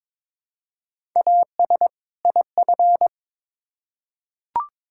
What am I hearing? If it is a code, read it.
Morse code, heard as a keyed tone, about 22 wpm, 700 Hz. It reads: AS IF
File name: Build A Phrase - 2-Word Phrases - 22wpm